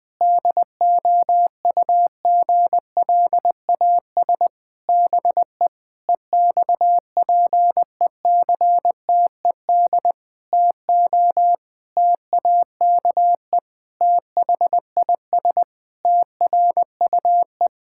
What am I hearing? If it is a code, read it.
Morse code, heard as a keyed tone, about 21 wpm, 700 Hz. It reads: DOUGLAS BE EXPECTED TO TAKE THIS TRUE
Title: Pangram Challenge v2 - 20wpm